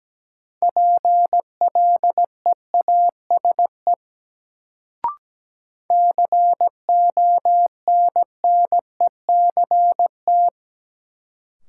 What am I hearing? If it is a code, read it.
Morse code, heard as a keyed tone, about 17 wpm, 700 Hz. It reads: PLEASE CONNECT
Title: Sentences from Top 1500 Words - New Word Drill Alt - 17wpm